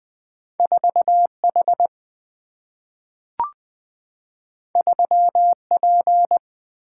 Morse code, heard as a keyed tone, about 20 wpm, 700 Hz. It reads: 4H 3P
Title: Single-Letter Single-Number 20wpm